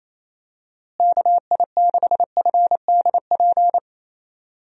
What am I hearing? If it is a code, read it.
Morse code, heard as a keyed tone, about 28 wpm, 700 Hz. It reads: KI6FDP
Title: USA Call Signs - 6 Characters - Repeated 3x - 28wpm